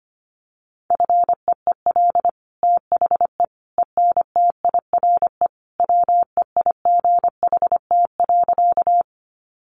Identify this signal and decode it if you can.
Morse code, heard as a keyed tone, about 25 wpm, 700 Hz. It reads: FEEL THE ENTIRE WEIGHT.
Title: Sentences from Top 800 Words 25wpm